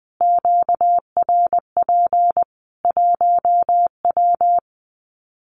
Morse code, heard as a keyed tone, about 20 wpm, 700 Hz. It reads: QRP 1W
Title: Common QSO Phrases 20wpm